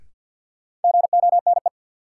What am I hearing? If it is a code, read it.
Morse code, heard as a keyed tone, about 46 wpm, 700 Hz. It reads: GONE
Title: Sentences from Top 1200 Words - New Word Drill - 50wpm